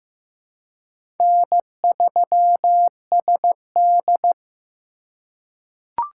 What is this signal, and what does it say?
Morse code, heard as a keyed tone, about 15 wpm, 700 Hz. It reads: N3SD
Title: USA Call Signs - 3x Character Spacing - 15wpm